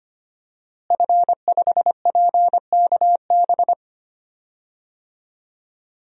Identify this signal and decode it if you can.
Morse code, heard as a keyed tone, about 25 wpm, 700 Hz. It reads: F5PKB
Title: DXCC-100 Call Signs - 5 Characters - Repeated 3x - 25wpm